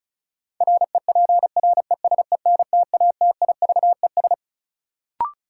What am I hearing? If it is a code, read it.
Morse code, heard as a keyed tone, about 34 wpm, 700 Hz. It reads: REPRESENTATIVES
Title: Sets of 5 Words Encore - New Word Drill - 35wpm